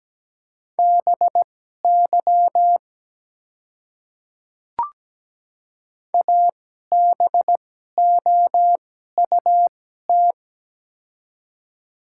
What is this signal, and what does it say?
Morse code, heard as a keyed tone, about 17 wpm, spaced as for 12 wpm, 700 Hz. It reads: BY ABOUT
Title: 2x Character Spacing - Top 100 Words - 17wpm